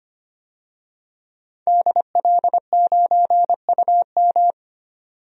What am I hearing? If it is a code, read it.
Morse code, heard as a keyed tone, about 25 wpm, 700 Hz. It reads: DL9UM
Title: DXCC-100 Call Signs - 5 Characters - 25wpm